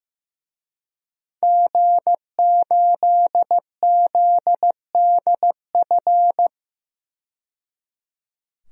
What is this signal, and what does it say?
Morse code, heard as a keyed tone, about 15 wpm, 700 Hz. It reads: G8ZDF
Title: DXCC-100 Call Signs - 5 Characters - 15wpm